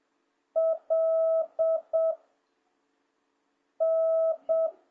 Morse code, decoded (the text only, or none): L N